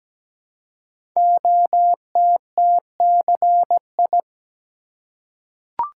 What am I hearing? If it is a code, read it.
Morse code, heard as a keyed tone, about 17 wpm, 700 Hz. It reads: OTTCI
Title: Code Groups - 5 Letters - 17wpm